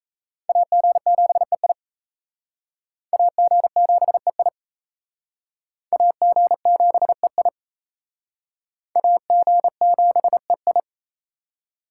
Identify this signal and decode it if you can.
Morse code, heard as a keyed tone, about 37 wpm, 700 Hz. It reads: AG7EI AG7EI AG7EI AG7EI